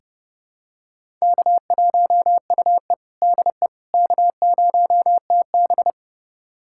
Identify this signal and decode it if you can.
Morse code, heard as a keyed tone, about 31 wpm, 700 Hz. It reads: K1UE DE K0TB
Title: USA Callsign DE Callsign - 4 Characters - 30wpm